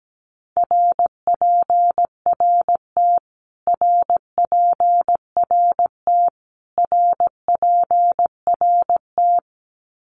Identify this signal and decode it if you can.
Morse code, heard as a keyed tone, about 17 wpm, 700 Hz. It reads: RPRT RPRT RPRT
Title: Abbreviations Repeated 3x - 17wpm